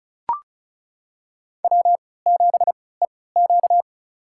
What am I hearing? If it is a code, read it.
Morse code, heard as a keyed tone, about 36 wpm, spaced as for 20 wpm, 700 Hz. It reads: W7EQ
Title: USA Call Signs - 3x Character Spacing - 35wpm